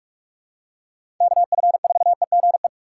Morse code, heard as a keyed tone, about 42 wpm, 700 Hz. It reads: KF4EGE
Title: USA Call Signs - 6 Characters - 4x Character Spacing - 45wpm